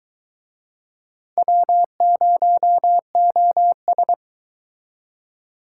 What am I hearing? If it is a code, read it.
Morse code, heard as a keyed tone, about 23 wpm, 700 Hz. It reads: W0OS